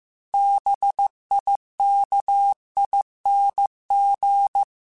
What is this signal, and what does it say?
Morse code, heard as a keyed tone, about 15 wpm, 800 Hz. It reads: BIKING